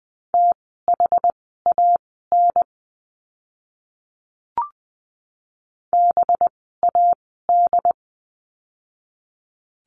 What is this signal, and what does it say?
Morse code, heard as a keyed tone, about 20 wpm, spaced as for 15 wpm, 700 Hz. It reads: THAN BAD
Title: 185 Lesson 30f - Letter b - Words 20wpm